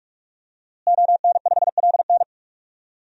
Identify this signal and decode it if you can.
Morse code, heard as a keyed tone, about 44 wpm, 700 Hz. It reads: ON5LN
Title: DXCC-100 Call Signs - 5 Characters - Repeated 2x - 45wpm